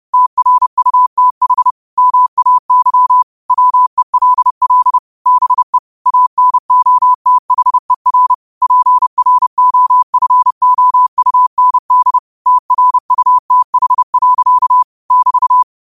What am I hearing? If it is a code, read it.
Morse code, heard as a keyed tone, about 30 wpm, 1000 Hz. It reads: TRUTH MAY WELL BE ANOTHER PROFOUND TRUTH. <BT>
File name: qotdfiles / 30wpm